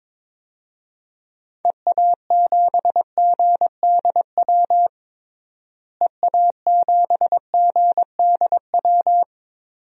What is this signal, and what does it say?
Morse code, heard as a keyed tone, about 22 wpm, 700 Hz. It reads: EA7GDW EA7GDW
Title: DXCC-100 Call Signs - 6 Characters - Repeated 2x - 22wpm